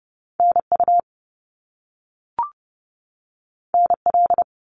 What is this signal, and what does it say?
Morse code, heard as a keyed tone, about 30 wpm, 700 Hz. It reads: NU NL
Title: Canadian Provinces and Territories 30wpm